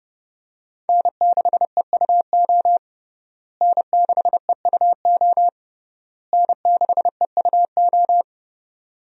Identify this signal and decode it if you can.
Morse code, heard as a keyed tone, about 30 wpm, 700 Hz. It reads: N6EUO N6EUO N6EUO
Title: USA Call Signs - 5 Characters - Repeated 3x - 30wpm